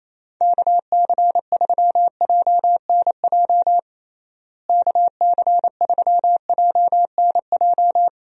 Morse code, heard as a keyed tone, about 28 wpm, 700 Hz. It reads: KC3JNJ KC3JNJ
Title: USA Call Signs - 6 Characters - Repeated 2x - 28wpm